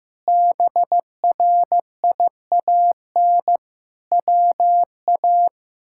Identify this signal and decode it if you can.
Morse code, heard as a keyed tone, about 15 wpm, 700 Hz. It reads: BRIAN WA
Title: K1USN SST QSO Practice - 15wpm